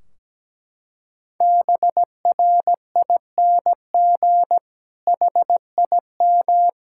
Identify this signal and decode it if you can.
Morse code, heard as a keyed tone, about 17 wpm, 700 Hz. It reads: BRING HIM
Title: Sentences from Top 1600 Words 17wpm